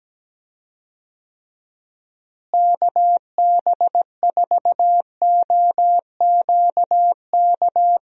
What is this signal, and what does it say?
Morse code, heard as a keyed tone, about 17 wpm, 700 Hz. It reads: KB4OQK
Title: USA Call Signs - 6 Characters - Repeated 3x - 17wpm